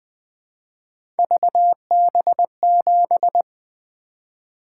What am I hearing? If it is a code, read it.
Morse code, heard as a keyed tone, about 20 wpm, 700 Hz. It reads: VB7
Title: Three Letter-Number Combinations - 20wpm